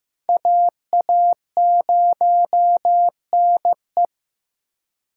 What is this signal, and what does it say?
Morse code, heard as a keyed tone, about 18 wpm, 700 Hz. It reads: AA0NE
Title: Speed Racing - USA 2x2 Call Signs 15wpm